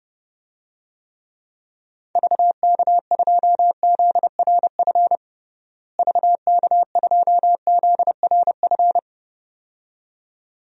Morse code, heard as a keyed tone, about 32 wpm, 700 Hz. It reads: VK2ZRF VK2ZRF
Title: DXCC-100 Call Signs - 6 Characters - Repeated 2x - 30wpm